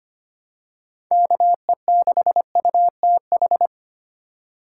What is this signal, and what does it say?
Morse code, heard as a keyed tone, about 25 wpm, 700 Hz. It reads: KE6UTH
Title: USA Call Signs - 6 Characters - 2x Character Spacing - 25wpm